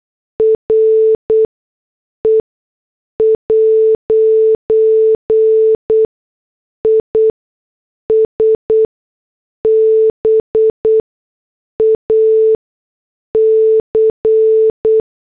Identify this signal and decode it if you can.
Morse code, heard as a keyed tone, about 8 wpm, spaced as for 6 wpm, 440 Hz. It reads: RE' IS BAC